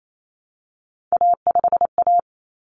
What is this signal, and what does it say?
Morse code, heard as a keyed tone, about 28 wpm, 700 Hz. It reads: A5A
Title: Common Cut Numbers - 3-Digit 28wpm